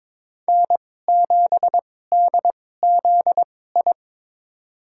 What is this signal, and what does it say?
Morse code, heard as a keyed tone, about 22 wpm, spaced as for 16 wpm, 700 Hz. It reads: N7DZI
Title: USA Call Signs - 2x Character Spacing - 22wpm